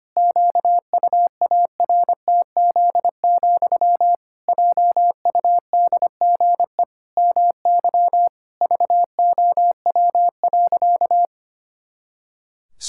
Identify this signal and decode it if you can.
Morse code, heard as a keyed tone, about 25 wpm, 700 Hz. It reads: QUARTZ, JUDGE MY VOW.
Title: Pangram Challenge v2 - 25wpm